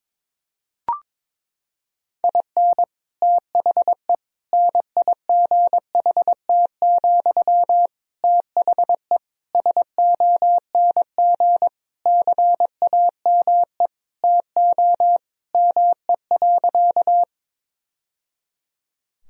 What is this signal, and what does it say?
Morse code, heard as a keyed tone, about 22 wpm, 700 Hz. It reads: IN THE NIGHT, THE SONG CAME TO ME.
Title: Sentences from Top 1000 Words - Review Part 2 - 22wpm